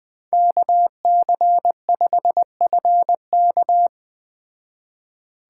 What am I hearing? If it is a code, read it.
Morse code, heard as a keyed tone, about 21 wpm, 700 Hz. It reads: KC5FK